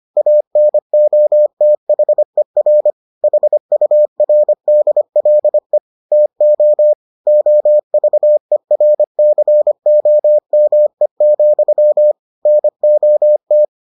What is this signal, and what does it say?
Morse code, heard as a keyed tone, about 25 wpm, 600 Hz. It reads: ANOTHER HURDLE TO OVERCOME, NOT